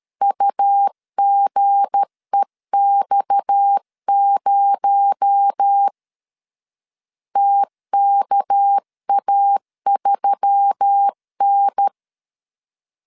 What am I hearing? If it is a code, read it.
Morse code, heard as a keyed tone, about 13 wpm, 800 Hz. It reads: UGEX0 TKA3N
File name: LEZIONI / 7 GF9Z / 7 G 12wpm 2055